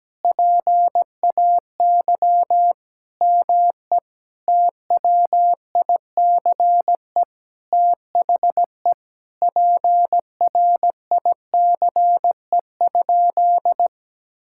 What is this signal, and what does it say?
Morse code, heard as a keyed tone, about 17 wpm, 700 Hz. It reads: PAY ME TWICE THE PRICE?
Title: Sentences from Top 1600 Words 17wpm